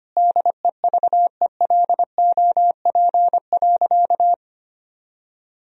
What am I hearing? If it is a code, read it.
Morse code, heard as a keyed tone, about 25 wpm, 700 Hz. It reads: DEVELOP.
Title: Sentences from Top 1000 Words - Review Part 2 - 25wpm